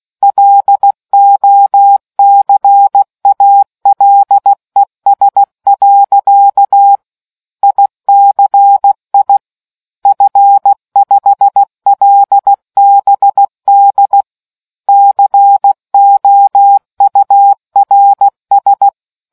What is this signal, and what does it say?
Morse code, heard as a keyed tone, about 16 wpm, 800 Hz. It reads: LOCALES. ICI F5LBD COURS